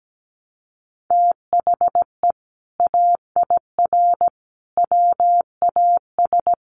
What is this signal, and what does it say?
Morse code, heard as a keyed tone, about 17 wpm, 700 Hz. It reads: THE AIR WAS
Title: Sentences from Top 1200 Words 17wpm